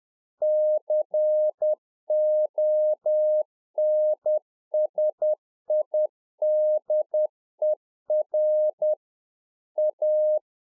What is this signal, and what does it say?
Morse code, heard as a keyed tone, about 10 wpm, 610 Hz. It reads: CONSIDER A